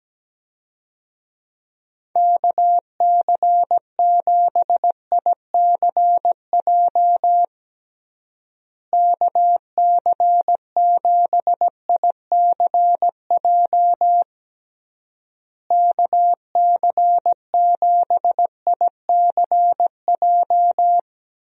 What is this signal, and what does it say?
Morse code, heard as a keyed tone, about 17 wpm, 700 Hz. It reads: KC7ICJ KC7ICJ KC7ICJ
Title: USA Call Signs - 6 Characters - Repeated 3x - 17wpm